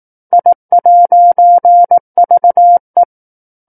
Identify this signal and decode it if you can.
Morse code, heard as a keyed tone, about 18 wpm, 700 Hz. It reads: I'VE
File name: 18 WPM